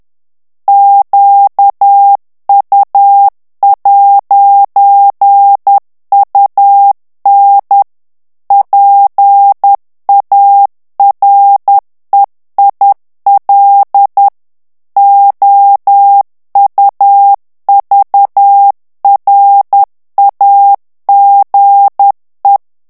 Morse code, heard as a keyed tone, about 10 wpm, 800 Hz. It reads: QU'UN PAREIL OUVRAGE
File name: morse mp3 / FRANCAIS TEXTE CLAIR 2 VITESSE 10 MOTS MINUTES